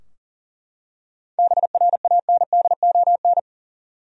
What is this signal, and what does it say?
Morse code, heard as a keyed tone, about 37 wpm, 700 Hz. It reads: BRANDON
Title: USA Common Names - 40wpm